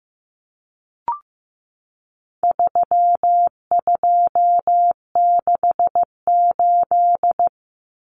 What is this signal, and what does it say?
Morse code, heard as a keyed tone, about 15 wpm, 700 Hz. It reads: 3268
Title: Numbers - 4-Digit 15wpm